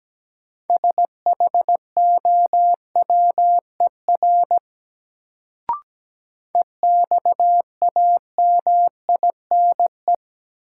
Morse code, heard as a keyed tone, about 17 wpm, 700 Hz. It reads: SHOWER EXAMINE
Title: Sentences from Top 1600 Words - New Word Drill - 17wpm